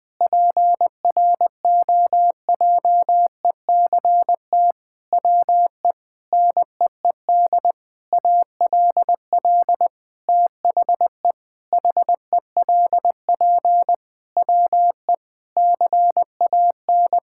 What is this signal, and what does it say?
Morse code, heard as a keyed tone, about 20 wpm, 700 Hz. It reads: PROJECT WE NEED ALL THE HELP WE CAN
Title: Sentences with 8-Word Phrases - Spoken Beginning - 20wpm